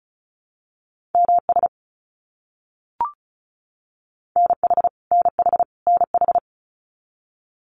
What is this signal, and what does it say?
Morse code, heard as a keyed tone, about 34 wpm, 700 Hz. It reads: MS NH NH NH
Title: US States Repeated 3x 35wpm